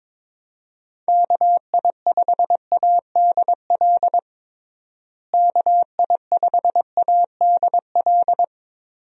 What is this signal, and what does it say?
Morse code, heard as a keyed tone, about 22 wpm, 700 Hz. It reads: KI5ADL KI5ADL
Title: USA Call Signs - 6 Characters - Repeated 3x - 22wpm